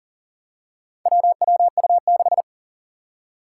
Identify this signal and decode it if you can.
Morse code, heard as a keyed tone, about 40 wpm, 700 Hz. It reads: WWU6
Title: Four Challenging Letter-Number Combinations 40wpm